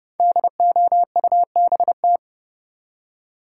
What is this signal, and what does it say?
Morse code, heard as a keyed tone, about 32 wpm, 700 Hz. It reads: DOUBT
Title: Five Letter Words - 30wpm